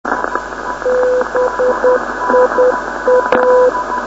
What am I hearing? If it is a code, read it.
Morse code, heard as a keyed tone, about 9 wpm, 500 Hz, with radio noise. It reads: BIA
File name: 472khz web swl / 05october2012 / dd7pc 472.5khz